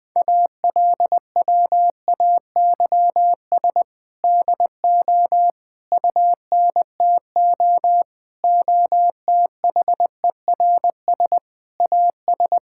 Morse code, heard as a keyed tone, about 20 wpm, 700 Hz. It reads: ALWAYS DO UNTO OTHERS AS
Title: Sentences with 8-Word Phrases - 20wpm